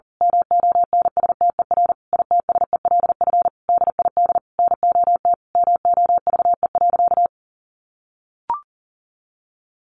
Morse code, heard as a keyed tone, about 40 wpm, 700 Hz. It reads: MONSTER ITSELF DID NOT MOVE.
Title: A Sound Of Thunder 40wpm